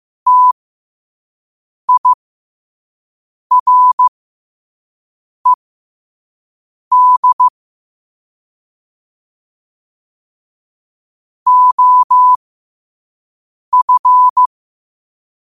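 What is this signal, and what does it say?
Morse code, heard as a keyed tone, about 15 wpm, spaced as for 5 wpm, 1000 Hz. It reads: TIRED OF
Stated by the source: Morse code sent at 5 wpm.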